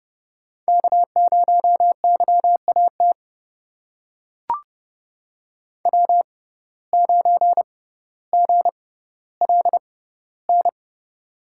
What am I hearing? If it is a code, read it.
Morse code, heard as a keyed tone, about 30 wpm, spaced as for 15 wpm, 700 Hz. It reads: K0YAT W9GLN
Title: USA Call Signs - 5 Characters - 6x Character Spacing - 30wpm